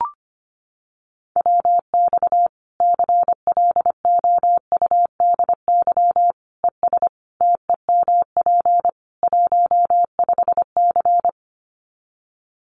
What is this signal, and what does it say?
Morse code, heard as a keyed tone, about 25 wpm, 700 Hz. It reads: WX CLOUDY ES TEMP 15C
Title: Common QSO Phrases 25wpm